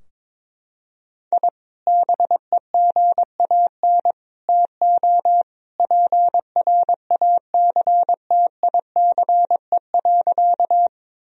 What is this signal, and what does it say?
Morse code, heard as a keyed tone, about 22 wpm, 700 Hz. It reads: I BEGAN TO PRACTICE.
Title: Sentences from Top 1000 Words - Review Part 4 - 22wpm